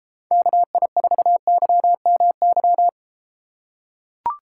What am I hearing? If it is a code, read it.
Morse code, heard as a keyed tone, about 32 wpm, 700 Hz. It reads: KI4YMY